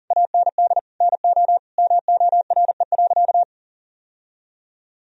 Morse code, heard as a keyed tone, about 40 wpm, 700 Hz. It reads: AND NO MORE.
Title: Sentences from Top 1000 Words - Review Part 1 - 40wpm